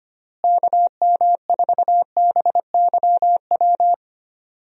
Morse code, heard as a keyed tone, about 25 wpm, 700 Hz. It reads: KM4BYW
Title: USA Call Signs - 6 Characters - Repeated 3x - 25wpm